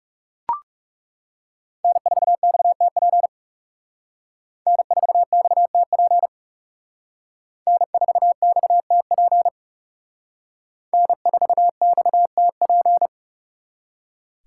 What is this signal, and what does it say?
Morse code, heard as a keyed tone, about 36 wpm, 700 Hz. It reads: N4XTP N4XTP N4XTP N4XTP
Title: Speed Racing - USA Call Signs 30wpm